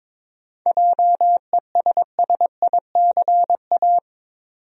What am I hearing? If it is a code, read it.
Morse code, heard as a keyed tone, about 21 wpm, 700 Hz. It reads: JESSICA
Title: USA Common Names - 22wpm